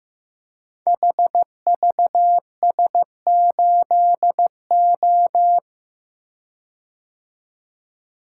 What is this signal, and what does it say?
Morse code, heard as a keyed tone, about 15 wpm, 700 Hz. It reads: HVS8O
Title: Code Groups - 5 Letter:Numbers - 15wpm